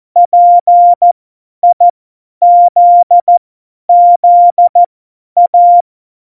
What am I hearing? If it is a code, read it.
Morse code, heard as a keyed tone, about 14 wpm, spaced as for 10 wpm, 700 Hz. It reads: PIZZA
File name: Morsekurs / Witze / Witz 06. Oktober 2020, 50 F70